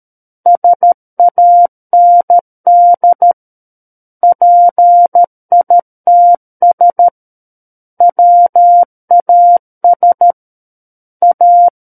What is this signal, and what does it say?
Morse code, heard as a keyed tone, about 13 wpm, 700 Hz. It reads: SAND PITS WAS A
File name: Book One / 04-Chapter 4 - 13 WPM